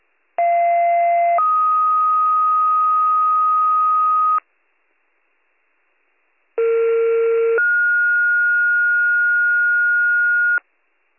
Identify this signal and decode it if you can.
Two-tone page, heard as a one-second tone, then a three-second tone, about 690 then 1250 Hz, about 460 then 1430 Hz. It is heard from 0.4 to 10.6 s.